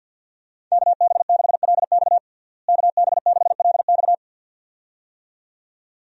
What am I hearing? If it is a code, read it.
Morse code, heard as a keyed tone, about 51 wpm, 700 Hz. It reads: KB6LX KB6LX
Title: USA Call Signs - 5 Characters - Repeated 2x - 50wpm